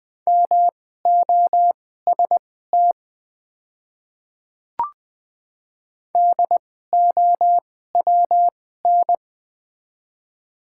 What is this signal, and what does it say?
Morse code, heard as a keyed tone, about 20 wpm, spaced as for 14 wpm, 700 Hz. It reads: MOST DOWN
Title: 2x Character Spacing - Top 100 Words - 20wpm